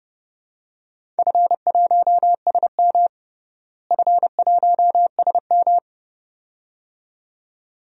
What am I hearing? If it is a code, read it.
Morse code, heard as a keyed tone, about 30 wpm, 700 Hz. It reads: F1SM F1SM
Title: DXCC-100 Call Signs - 4 Characters - Repeated 2x - 30wpm